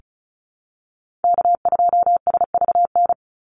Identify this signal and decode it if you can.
Morse code, heard as a keyed tone, about 34 wpm, 700 Hz. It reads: K2SVN
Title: USA Call Signs 35wpm